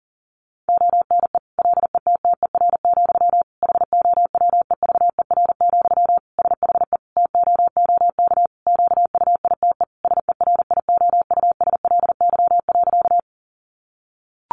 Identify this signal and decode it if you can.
Morse code, heard as a keyed tone, about 40 wpm, 700 Hz. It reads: ONE LETTER, HOWEVER, SHE TOOK QUITE SERIOUSLY.
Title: Sentences from Top 1200 Words 40wpm